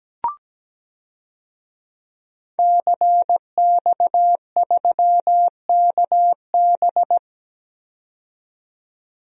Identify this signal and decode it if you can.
Morse code, heard as a keyed tone, about 17 wpm, 700 Hz. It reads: CX3KB
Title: DXCC-100 Call Signs - 5 Characters - 17wpm